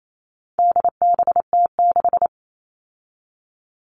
Morse code, heard as a keyed tone, about 28 wpm, 700 Hz. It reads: DBT6
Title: All Cut Numbers - 4-Digit 28wpm